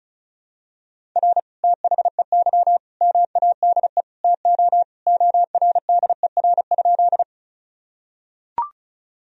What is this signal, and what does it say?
Morse code, heard as a keyed tone, about 36 wpm, 700 Hz. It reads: R THEY MADE TO ORDER?
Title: Sentences from Top 400 Words and Abbreviations 35wpm